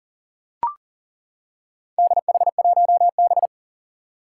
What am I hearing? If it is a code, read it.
Morse code, heard as a keyed tone, about 40 wpm, 700 Hz. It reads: DH1B
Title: Four Challenging Letter-Number Combinations 40wpm